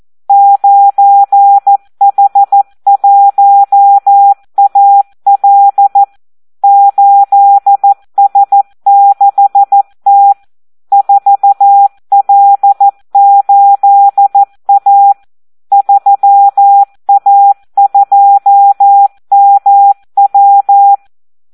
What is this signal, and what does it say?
Morse code, heard as a keyed tone, about 14 wpm, 800 Hz. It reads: 9H1AL 8S6T 4L8A 3A2MW
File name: Audio practica CW / SD 1 14